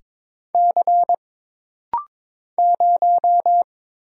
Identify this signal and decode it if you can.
Morse code, heard as a keyed tone, about 22 wpm, 700 Hz. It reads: C 0
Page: This rapid-fire timing is geared to push your ICR (Instant Character Recognition) skills.